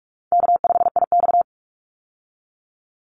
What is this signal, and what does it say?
Morse code, heard as a keyed tone, about 44 wpm, 700 Hz. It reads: K5IX
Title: Call Signs Repeated 2x - 45wpm